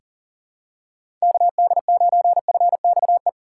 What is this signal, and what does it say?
Morse code, heard as a keyed tone, about 37 wpm, 700 Hz. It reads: KD9FXE
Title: USA Call Signs - 4x Character Spacing - 40wpm